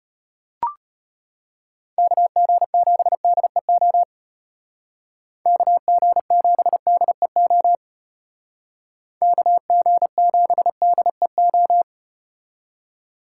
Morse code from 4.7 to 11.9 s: KG7DEO KG7DEO

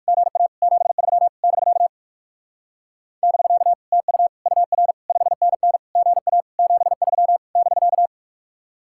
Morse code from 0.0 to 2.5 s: GA 73 <BK>

